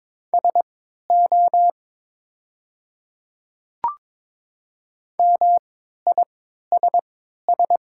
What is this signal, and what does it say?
Morse code, heard as a keyed tone, about 22 wpm, spaced as for 13 wpm, 690 Hz. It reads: SO MISS
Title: 3x Character Spacing - Top 500 Words - 22wpm